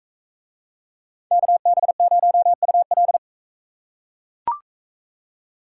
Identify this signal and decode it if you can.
Morse code, heard as a keyed tone, about 41 wpm, 700 Hz. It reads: KB0UL